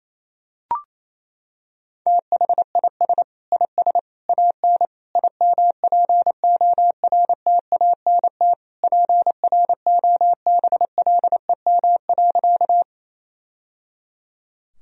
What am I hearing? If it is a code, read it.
Morse code, heard as a keyed tone, about 28 wpm, 700 Hz. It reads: THIS IS AN IMPORTANT PROBLEM.
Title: Sentences from Top 1000 Words - Review Part 2 - 28wpm